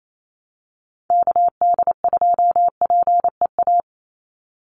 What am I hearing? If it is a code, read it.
Morse code, heard as a keyed tone, about 28 wpm, 700 Hz. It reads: KD2PEA
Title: USA Call Signs 28wpm